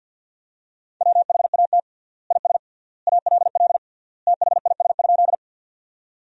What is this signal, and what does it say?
Morse code, heard as a keyed tone, about 47 wpm, 700 Hz. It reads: WHAT IS ALL THIS?
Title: 3x Word-Spacing - Sentences from Top 100 Words - 50wpm